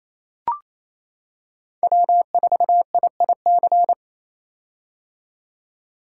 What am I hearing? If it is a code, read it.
Morse code, heard as a keyed tone, about 27 wpm, 700 Hz. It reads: W4IIC
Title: K1USN SST QSO Practice - 28wpm